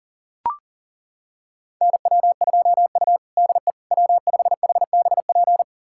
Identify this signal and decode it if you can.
Morse code, heard as a keyed tone, about 37 wpm, 700 Hz. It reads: NW2U DE W5HBP
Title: USA Callsign DE Callsign 40wpm